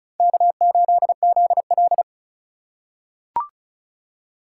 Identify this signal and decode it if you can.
Morse code, heard as a keyed tone, about 36 wpm, 700 Hz. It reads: K8ZL
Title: USA Call Signs - 4x Character Spacing - 35wpm